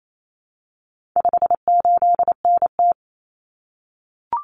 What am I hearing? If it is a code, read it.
Morse code, heard as a keyed tone, about 28 wpm, 700 Hz. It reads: 58NT
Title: Common Cut Numbers - 4-Digit 28wpm